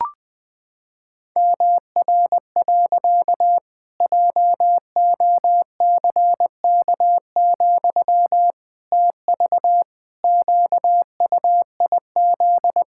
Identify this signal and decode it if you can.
Morse code, heard as a keyed tone, about 21 wpm, 700 Hz. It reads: MR. JOCK, TV QUIZ
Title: Pangram Challenge v2 - 20wpm